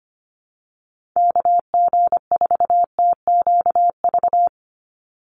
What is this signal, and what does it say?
Morse code, heard as a keyed tone, about 25 wpm, 700 Hz. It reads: KG4TQV
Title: USA Call Signs 25wpm